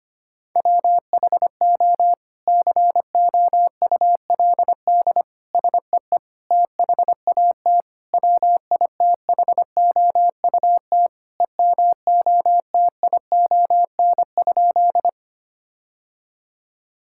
Morse code, heard as a keyed tone, about 25 wpm, 700 Hz. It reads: WHO COULD SEE THAT WITHOUT EMOTION?
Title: Sentences from Top 1600 Words 25wpm